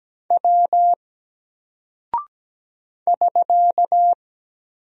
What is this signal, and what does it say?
Morse code, heard as a keyed tone, about 17 wpm, 700 Hz. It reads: W <SK>
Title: Single Character (Letter:Number:Punctuation:Prosign) 17wpm